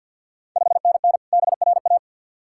Morse code, heard as a keyed tone, about 51 wpm, 700 Hz. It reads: 5NN BRA
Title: Ontario QSO Party Exchange 50wpm